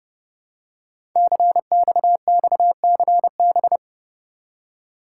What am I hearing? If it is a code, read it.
Morse code, heard as a keyed tone, about 30 wpm, 700 Hz. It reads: CXXCB
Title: Code Groups - 5 Letters - 30wpm